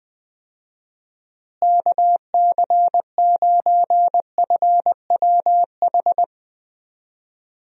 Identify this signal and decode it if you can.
Morse code, heard as a keyed tone, about 20 wpm, 700 Hz. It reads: KC9FWH
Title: USA Call Signs - 6 Characters - Repeated 2x - 20wpm